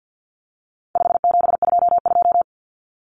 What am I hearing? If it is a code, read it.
Morse code, heard as a keyed tone, about 50 wpm, 700 Hz. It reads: 5722
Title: Common Cut Numbers - 4-Digit 50wpm